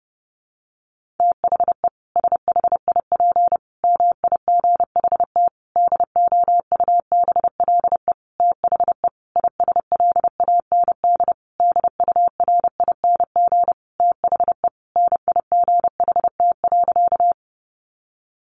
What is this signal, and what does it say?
Morse code, heard as a keyed tone, about 30 wpm, 700 Hz. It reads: THE SHIP MIGHT DOUBLE THE ISLAND DURING THE NIGHT.
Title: Sentences from Top 1400 Words 30wpm